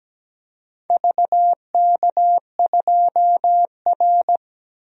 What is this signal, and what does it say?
Morse code, heard as a keyed tone, about 17 wpm, 700 Hz. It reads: VK2R
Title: DXCC-100 Call Signs - 4 Characters - Repeated 3x - 17wpm